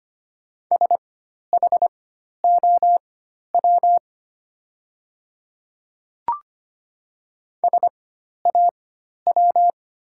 Morse code, heard as a keyed tone, about 25 wpm, spaced as for 12 wpm, 700 Hz. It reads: SHOW SAW
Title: 4x Character Spacing - Top 500 Words - 25wpm